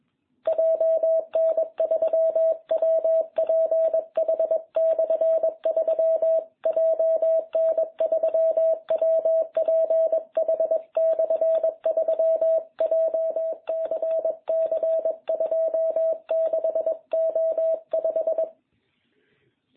Morse code, heard as a keyed tone, about 21 wpm, 660 Hz. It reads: JN3WPH/3 JN3WPH/3 JCC 26O5